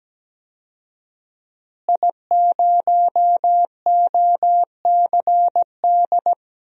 Morse code, heard as a keyed tone, about 17 wpm, 700 Hz. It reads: I0OCD
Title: DXCC-100 Call Signs - 5 Characters - Repeated 2x - 17wpm